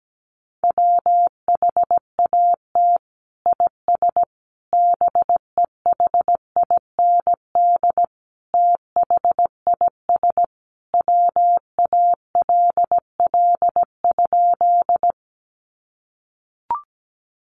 Morse code, heard as a keyed tone, about 17 wpm, 700 Hz. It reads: WHAT IS BEHIND THIS WALL?
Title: Sentences from Top 1100 Words 17wpm